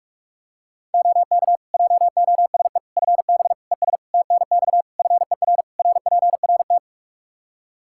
Morse code, heard as a keyed tone, about 46 wpm, 700 Hz. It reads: OK JOSE FB ES TNX FER RPRT
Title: K1USN SST QSO Practice 45wpm